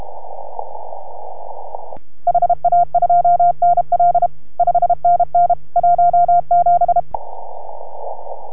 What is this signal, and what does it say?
Morse code, heard as a keyed tone, about 31 wpm, 690 Hz. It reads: HA2NL 5NN 17